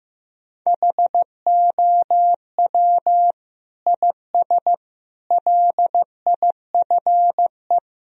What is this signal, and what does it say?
Morse code, heard as a keyed tone, about 15 wpm, 700 Hz. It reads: HOW IS LIFE
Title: Sentences from Top 1000 Words - Review Part 2 - 15wpm